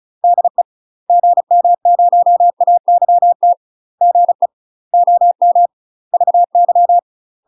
Morse code, heard as a keyed tone, about 34 wpm, 700 Hz. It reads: DE GM0AYT GE OM VY
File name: UK MASTER QSOs TRAINING / 35WPM / UK MASTER QSOs 0024